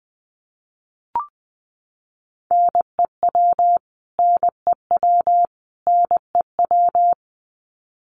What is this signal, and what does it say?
Morse code, heard as a keyed tone, about 20 wpm, 700 Hz. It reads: NEW NEW NEW
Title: Top 500 Words Repeated 20wpm